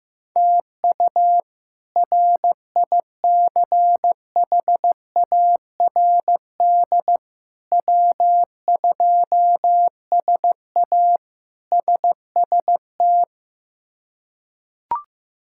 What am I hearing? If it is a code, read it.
Morse code, heard as a keyed tone, about 15 wpm, 700 Hz. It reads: TU RICHARD W2SA SST
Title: K1USN SST QSO Practice - 15wpm